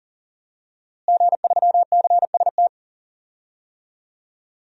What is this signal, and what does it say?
Morse code, heard as a keyed tone, about 37 wpm, 700 Hz. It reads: G3CST
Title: DXCC-100 Call Signs - 5 Characters - Repeated 3x - 40wpm